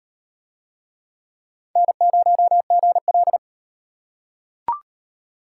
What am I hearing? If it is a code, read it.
Morse code, heard as a keyed tone, about 37 wpm, 700 Hz. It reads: N0GL